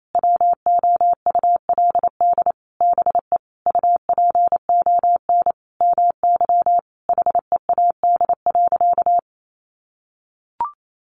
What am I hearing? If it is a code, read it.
Morse code, heard as a keyed tone, about 28 wpm, 700 Hz. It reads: WOULD BE UPON MY HEAD.
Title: Sentences from Top 1100 Words 28wpm